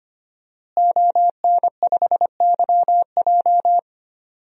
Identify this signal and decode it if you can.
Morse code, heard as a keyed tone, about 25 wpm, 700 Hz. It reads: ON5YJ
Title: DXCC-100 Call Signs - 5 Characters - Repeated 3x - 25wpm